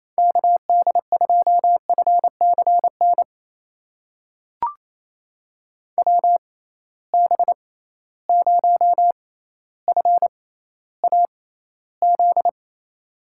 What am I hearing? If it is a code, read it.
Morse code, heard as a keyed tone, about 28 wpm, spaced as for 14 wpm, 700 Hz. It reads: KD2FCN WB0FAZ
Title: USA Call Signs - 6 Characters - 6x Character Spacing - 28wpm